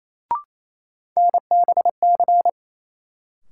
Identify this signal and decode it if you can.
Morse code, heard as a keyed tone, about 28 wpm, 700 Hz. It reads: NBC
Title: Sets of 5 Words Encore - New Word Drill - 28wpm